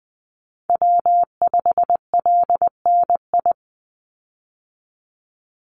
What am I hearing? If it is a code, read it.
Morse code, heard as a keyed tone, about 20 wpm, 700 Hz. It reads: W5LNI
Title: Call Signs - 20wpm